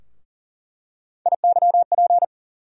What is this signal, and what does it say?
Morse code, heard as a keyed tone, about 40 wpm, 700 Hz. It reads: IYP
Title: USA Call Sign Postfixes - 3 Letters 40wpm